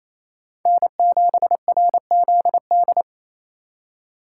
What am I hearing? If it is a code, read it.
Morse code, heard as a keyed tone, about 28 wpm, 700 Hz. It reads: N7RZD
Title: USA Call Signs - 5 Characters - 2x Character Spacing - 28wpm